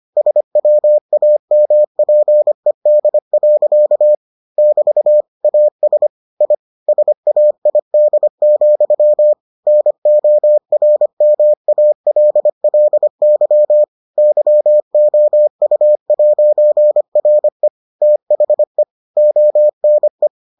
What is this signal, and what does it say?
Morse code, heard as a keyed tone, about 25 wpm, 600 Hz. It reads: SWAMPED. <BT> AS I SAID, NORMALLY YOU'RE THE ONE